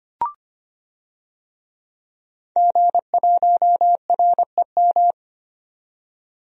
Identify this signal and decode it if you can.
Morse code, heard as a keyed tone, about 25 wpm, 700 Hz. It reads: G1REM
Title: DXCC-100 Call Signs - 5 Characters - 25wpm